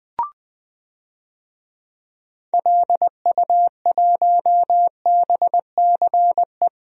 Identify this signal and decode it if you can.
Morse code, heard as a keyed tone, about 20 wpm, 700 Hz. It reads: LU1BCE
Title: DXCC-100 Call Signs - 6 Characters - 20wpm